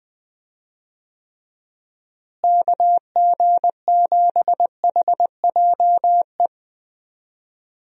Morse code, heard as a keyed tone, about 20 wpm, 700 Hz. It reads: KG7HJE
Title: USA Call Signs - 6 Characters - Repeated 3x - 20wpm